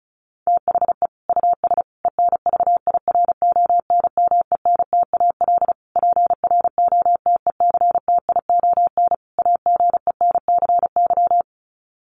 Morse code, heard as a keyed tone, about 34 wpm, 700 Hz. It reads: THE US ENVIRONMENTAL PROTECTION AGENCY
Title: Sets of 5 Words Encore2 35wpm